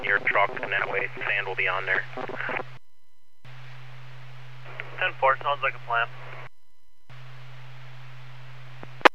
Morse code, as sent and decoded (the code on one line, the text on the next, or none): none